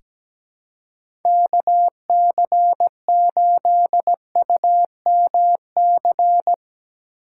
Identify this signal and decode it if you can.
Morse code, heard as a keyed tone, about 17 wpm, 700 Hz. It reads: KC8UMC
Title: USA Call Signs - 6 Characters - 4x Character Spacing - 17wpm